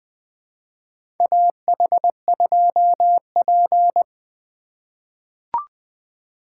Morse code, heard as a keyed tone, about 20 wpm, 700 Hz. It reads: AH2P
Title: USA Call Signs - 4 Characters - 4x Character Spacing - 20wpm